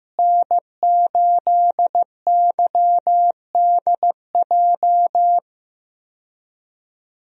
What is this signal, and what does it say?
Morse code, heard as a keyed tone, about 15 wpm, 700 Hz. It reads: N8YDJ
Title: USA Call Signs - 5 Characters - Repeated 3x - 15wpm